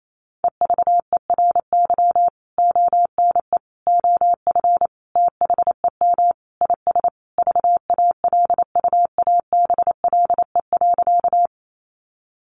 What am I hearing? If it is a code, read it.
Morse code, heard as a keyed tone, about 28 wpm, 700 Hz. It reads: EVERY ONE OF THEM IS VALUABLE.
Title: Sentences from Top 1100 Words 28wpm